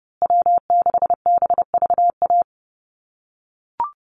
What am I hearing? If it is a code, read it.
Morse code, heard as a keyed tone, about 30 wpm, 700 Hz. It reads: W6BVA
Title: USA Call Signs Repeated 2x 30wpm